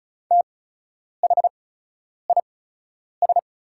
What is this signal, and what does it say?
Morse code, heard as a keyed tone, about 36 wpm, spaced as for 10 wpm, 700 Hz. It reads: THIS
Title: 8x Character Spacing - Top 100 Words - 35wpm